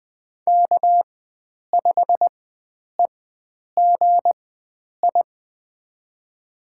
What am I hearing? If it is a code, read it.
Morse code, heard as a keyed tone, about 20 wpm, spaced as for 9 wpm, 700 Hz. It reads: K5EGI
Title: USA Call Signs - 5 Characters - 4x Character Spacing - 20wpm